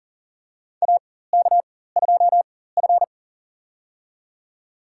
Morse code, heard as a keyed tone, about 40 wpm, spaced as for 19 wpm, 700 Hz. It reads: AK2F
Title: USA Call Signs - 4x Character Spacing - 40wpm